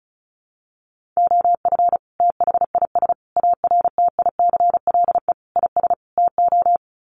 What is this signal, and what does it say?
Morse code, heard as a keyed tone, about 34 wpm, 700 Hz. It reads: OF THIS ARTICLE IS TO
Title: Sets of 5 Words Encore2 35wpm